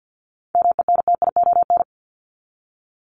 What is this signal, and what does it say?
Morse code, heard as a keyed tone, about 50 wpm, 700 Hz. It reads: MENTION
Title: ED Suffix - Top 100 Words with Base Words 50wpm